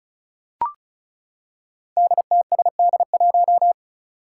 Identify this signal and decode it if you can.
Morse code, heard as a keyed tone, about 34 wpm, 700 Hz. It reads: DTSD1
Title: Five Challenging Letter-Number Combinations 35wpm